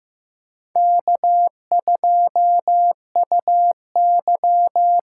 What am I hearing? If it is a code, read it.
Morse code, heard as a keyed tone, about 18 wpm, 700 Hz. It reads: K2UY
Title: Speed Racing - USA 1x2 Call Signs 15wpm